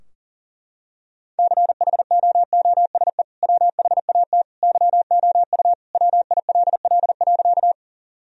Morse code, heard as a keyed tone, about 40 wpm, 700 Hz. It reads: CHOOSE WHAT YOU WILL.
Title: Sentences from Top 1000 Words - Review Part 4 - 40wpm